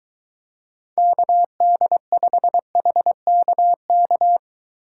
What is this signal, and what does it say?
Morse code, heard as a keyed tone, about 23 wpm, 700 Hz. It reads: KD5HKK